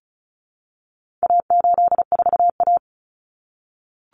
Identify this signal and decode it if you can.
Morse code, heard as a keyed tone, about 34 wpm, 700 Hz. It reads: A84A
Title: Common Cut Numbers - 4-Digit 35wpm